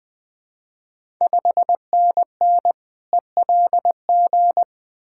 Morse code, heard as a keyed tone, about 20 wpm, 690 Hz. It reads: 5NN ELG
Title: Ontario QSO Party Exchange 20wpm